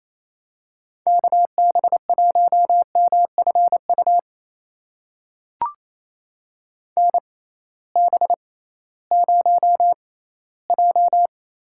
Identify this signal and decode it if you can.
Morse code, heard as a keyed tone, about 28 wpm, spaced as for 16 wpm, 700 Hz. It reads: KB1MFU NB0J
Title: USA Call Signs - 6x Character Spacing - 28wpm